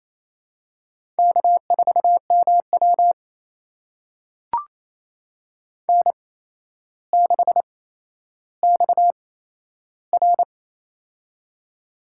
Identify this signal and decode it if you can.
Morse code, heard as a keyed tone, about 27 wpm, spaced as for 12 wpm, 700 Hz. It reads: K4MW N6XR
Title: USA Call Signs - 8x Character Spacing - 28wpm